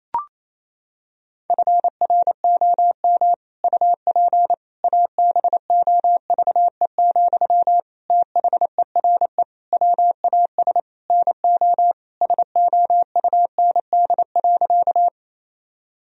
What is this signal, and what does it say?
Morse code, heard as a keyed tone, about 28 wpm, 700 Hz. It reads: FROM UP ABOVE, THERE WAS NO SOUND.
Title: Sentences from Top 1000 Words - Review Part 2 - 28wpm